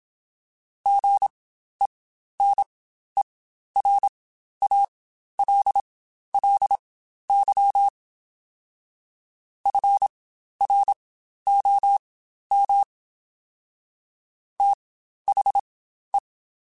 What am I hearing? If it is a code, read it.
Morse code, heard as a keyed tone, about 27 wpm, spaced as for 12 wpm, 800 Hz. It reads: GENERALLY FROM THE
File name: Spaced Characters / Spaced Characters-mp3s / Weather 5 1 sp @25